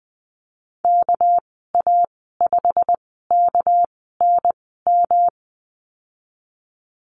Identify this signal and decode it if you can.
Morse code, heard as a keyed tone, about 20 wpm, spaced as for 14 wpm, 700 Hz. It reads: KA5KNM
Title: 200 Lesson 32g - Letter k - Call Signs 20wpm